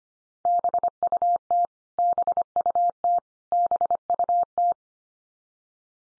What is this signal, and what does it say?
Morse code, heard as a keyed tone, about 25 wpm, 700 Hz. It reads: BUT BUT BUT
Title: Top 100 Words 25wpm x3